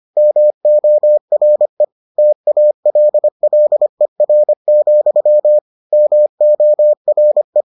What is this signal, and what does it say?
Morse code, heard as a keyed tone, about 25 wpm, 600 Hz. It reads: MORE TALLER, MORE